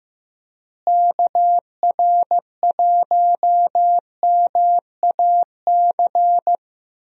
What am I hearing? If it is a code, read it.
Morse code, heard as a keyed tone, about 15 wpm, 700 Hz. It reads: KR1MAC
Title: USA Call Signs - 6 Characters - Repeated 3x - 15wpm